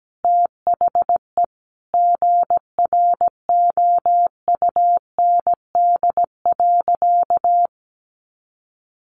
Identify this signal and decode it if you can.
Morse code, heard as a keyed tone, about 17 wpm, 700 Hz. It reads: THE GROUND.
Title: Sentences from Top 1500 Words 17wpm